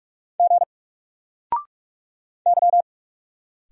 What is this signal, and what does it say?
Morse code, heard as a keyed tone, about 44 wpm, 700 Hz. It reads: G Y
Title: Single Character (Letter:Number:Punctuation:Prosign) 45wpm